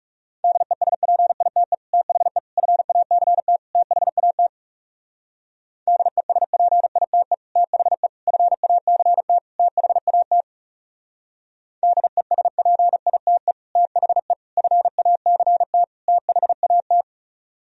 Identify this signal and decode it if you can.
Morse code, heard as a keyed tone, about 34 wpm, 700 Hz. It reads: DESPITE THE FACT THAT DESPITE THE FACT THAT DESPITE THE FACT THAT
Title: Speed-Racing - Sets of 4 Words 30wpm